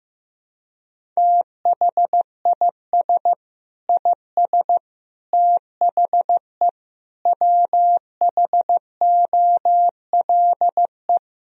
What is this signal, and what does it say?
Morse code, heard as a keyed tone, about 16 wpm, 700 Hz. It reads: THIS IS THE WHOLE
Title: Sentences from Top 1000 Words - Review Part 1 - 15wpm